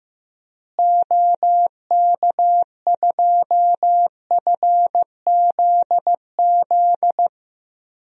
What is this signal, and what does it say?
Morse code, heard as a keyed tone, about 15 wpm, 700 Hz. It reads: OK2FZZ
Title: DXCC-100 Call Signs - 6 Characters - Repeated 3x - 15wpm